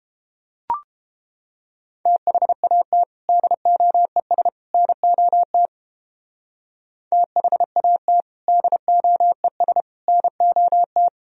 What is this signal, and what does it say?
Morse code, heard as a keyed tone, about 31 wpm, 700 Hz. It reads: THAT DOES NOT THAT DOES NOT